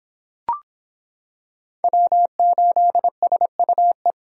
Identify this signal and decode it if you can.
Morse code, heard as a keyed tone, about 26 wpm, 700 Hz. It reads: W8SUE